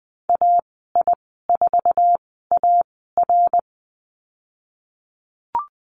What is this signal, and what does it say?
Morse code, heard as a keyed tone, about 20 wpm, spaced as for 15 wpm, 700 Hz. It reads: AI4AR
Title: 069 Lesson 12g - Letter d - Call Signs 20wpm